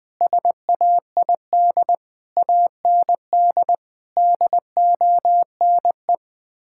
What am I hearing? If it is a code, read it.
Morse code, heard as a keyed tone, about 21 wpm, 700 Hz. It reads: SAID AND DONE
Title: Sets of 6 Words - 20wpm